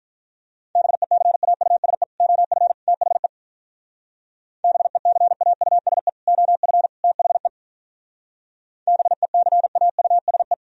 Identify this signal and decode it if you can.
Morse code, heard as a keyed tone, about 43 wpm, 700 Hz. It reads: BECAUSE OF THE BECAUSE OF THE BECAUSE
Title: Speed-Racing - Sets of 3 Words 35wpm